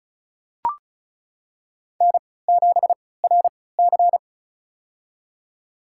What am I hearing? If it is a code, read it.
Morse code, heard as a keyed tone, about 34 wpm, spaced as for 20 wpm, 700 Hz. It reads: N7RC
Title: USA Call Signs - 4 Characters - 3x Character Spacing - 35wpm